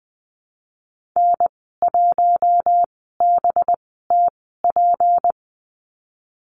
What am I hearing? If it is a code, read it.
Morse code, heard as a keyed tone, about 20 wpm, spaced as for 15 wpm, 700 Hz. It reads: N1BTP
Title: 231 Lesson 37g - Letter x - Call Signs 20wpm